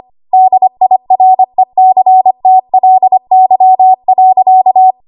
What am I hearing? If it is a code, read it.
Morse code, heard as a keyed tone, about 25 wpm, 750 Hz. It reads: DIRECTLY.
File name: Archive / 25 WPM / 25WPM